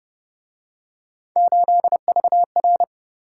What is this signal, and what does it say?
Morse code, heard as a keyed tone, about 29 wpm, 700 Hz. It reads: 8VR